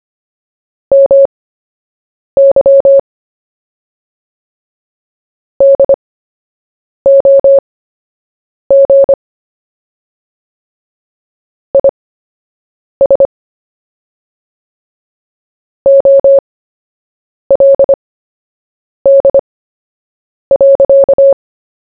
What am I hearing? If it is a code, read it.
Morse code, heard as a keyed tone, about 25 wpm, spaced as for 7 wpm, 550 Hz. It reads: MY DOG IS OLD.
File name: cw / fists / Short-Story-1-7wpm